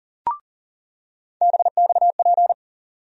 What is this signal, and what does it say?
Morse code, heard as a keyed tone, about 38 wpm, 700 Hz. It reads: BXP
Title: USA Call Sign Postfixes - 3 Letters 40wpm